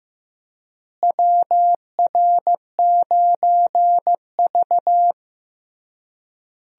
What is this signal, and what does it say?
Morse code, heard as a keyed tone, about 16 wpm, 700 Hz. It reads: WR9V
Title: USA Call Signs - 4 Characters - Repeated 3x - 15wpm